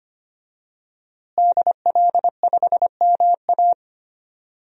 Morse code, heard as a keyed tone, about 25 wpm, 700 Hz. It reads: DL5MA
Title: DXCC-100 Call Signs - 5 Characters - Repeated 3x - 25wpm